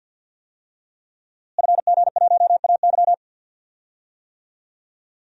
Morse code, heard as a keyed tone, about 46 wpm, 700 Hz. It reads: FG1AY
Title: DXCC-100 Call Signs - 5 Characters - Repeated 3x - 50wpm